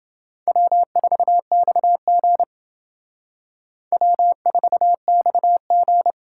Morse code, heard as a keyed tone, about 28 wpm, 700 Hz. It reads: W4XG W4XG